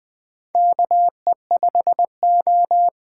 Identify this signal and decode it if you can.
Morse code, heard as a keyed tone, about 20 wpm, 700 Hz. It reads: KE5O
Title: USA Call Signs - 8x Character Spacing - 20wpm